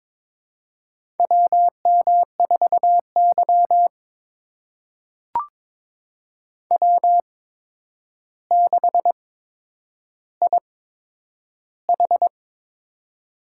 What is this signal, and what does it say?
Morse code, heard as a keyed tone, about 22 wpm, spaced as for 9 wpm, 700 Hz. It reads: WM4Y W6IH
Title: USA Call Signs - 8x Character Spacing - 22wpm